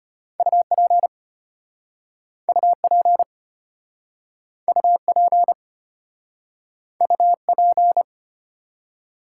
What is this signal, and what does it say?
Morse code, heard as a keyed tone, about 34 wpm, 700 Hz. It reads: UP UP UP UP